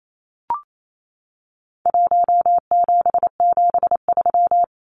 Morse code, heard as a keyed tone, about 28 wpm, 700 Hz. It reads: 1773
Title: Numbers - Contest Style - 4-Digit 28wpm